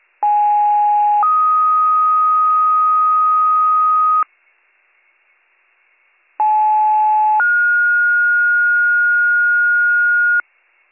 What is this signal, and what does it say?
Two-tone page, heard as a one-second tone, then a three-second tone, about 830 then 1250 Hz, about 860 then 1430 Hz. It is heard from 0.2 to 10.4 s.